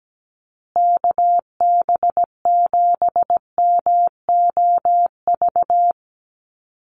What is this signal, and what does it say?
Morse code, heard as a keyed tone, about 17 wpm, 700 Hz. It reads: KB7MOV
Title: USA Call Signs Repeated 3x 17wpm